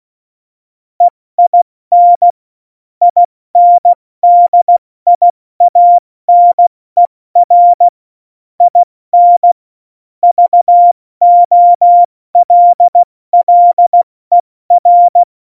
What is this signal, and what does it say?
Morse code, heard as a keyed tone, about 16 wpm, 700 Hz. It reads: EIN INDIANER IN VOLLER
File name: Morsekurs / Witze / Witz 04. September 2018, 70 F80